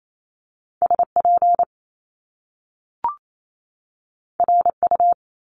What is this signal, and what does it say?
Morse code, heard as a keyed tone, about 28 wpm, 700 Hz. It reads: SP RU
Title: Top Two Letter Combinations in Words 28wpm